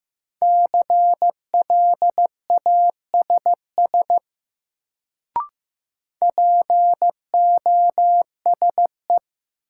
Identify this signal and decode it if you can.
Morse code, heard as a keyed tone, about 15 wpm, 700 Hz. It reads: CLASS POSE
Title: Sentences from Top 1000 Words - Review Part 2 - New Word Drill - 15wpm